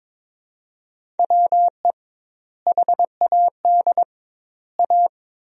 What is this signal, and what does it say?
Morse code, heard as a keyed tone, about 22 wpm, 700 Hz. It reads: WE HAD A
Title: 2x Word-Spacing - Sentences from Top 400 Words - 22wpm